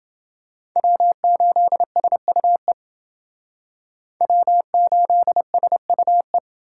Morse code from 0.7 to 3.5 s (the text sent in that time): W8SUE